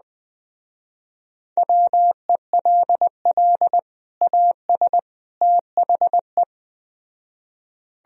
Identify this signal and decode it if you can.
Morse code, heard as a keyed tone, about 20 wpm, 700 Hz. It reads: WELL AS THE